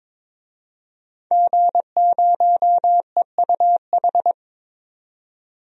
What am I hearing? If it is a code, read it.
Morse code, heard as a keyed tone, about 22 wpm, 700 Hz. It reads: G0EUH
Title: DXCC-100 Call Signs - 5 Characters - 22wpm